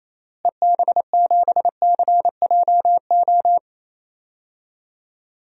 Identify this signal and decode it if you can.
Morse code, heard as a keyed tone, about 27 wpm, 700 Hz. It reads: EB7CJO
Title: DXCC-100 Call Signs - 6 Characters - Repeated 3x - 28wpm